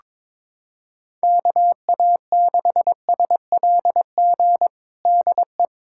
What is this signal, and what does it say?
Morse code, heard as a keyed tone, about 22 wpm, 700 Hz. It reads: KA6SLG DE
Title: USA Callsign DE Callsign - 6 Characters - 22wpm